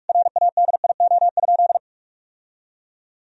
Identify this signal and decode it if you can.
Morse code, heard as a keyed tone, about 42 wpm, 700 Hz. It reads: RADIO?
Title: Sentences from Top 1000 Words - Review Part 3 - 45wpm